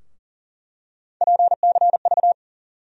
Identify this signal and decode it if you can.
Morse code, heard as a keyed tone, about 40 wpm, 700 Hz. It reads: PCV
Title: USA Call Sign Postfixes - 3 Letters 40wpm